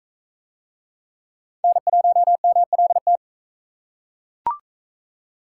Morse code, heard as a keyed tone, about 41 wpm, 700 Hz. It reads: N1MLT